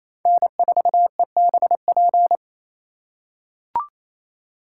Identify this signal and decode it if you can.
Morse code, heard as a keyed tone, about 28 wpm, 700 Hz. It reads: N4EBP
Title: USA Call Signs - 3x Character Spacing - 28wpm